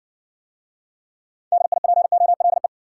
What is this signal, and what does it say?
Morse code, heard as a keyed tone, about 60 wpm, 700 Hz. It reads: DIPOLE